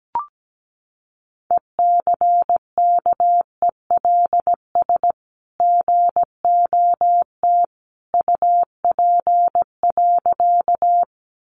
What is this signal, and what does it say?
Morse code, heard as a keyed tone, about 17 wpm, 700 Hz. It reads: ECKELS GOT UP.
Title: A Sound Of Thunder 17wpm